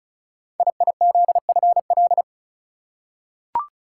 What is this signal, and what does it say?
Morse code, heard as a keyed tone, about 36 wpm, 700 Hz. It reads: IIZFL
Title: Code Groups - 5 Letters - 35wpm